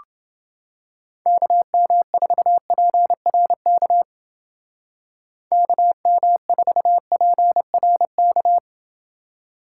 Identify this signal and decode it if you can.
Morse code, heard as a keyed tone, about 28 wpm, 700 Hz. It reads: KM4PRK KM4PRK